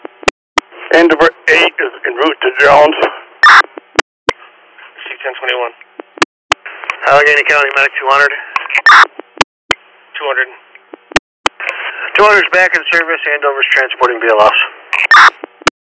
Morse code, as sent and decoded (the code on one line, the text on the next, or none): none